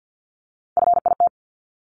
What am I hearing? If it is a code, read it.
Morse code, heard as a keyed tone, about 50 wpm, 700 Hz. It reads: FIT
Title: Top 751-1000 Words 50wpm